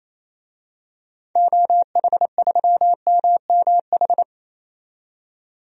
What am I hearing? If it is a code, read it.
Morse code, heard as a keyed tone, about 28 wpm, 700 Hz. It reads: OH3MMH
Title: DXCC-100 Call Signs - 6 Characters - 28wpm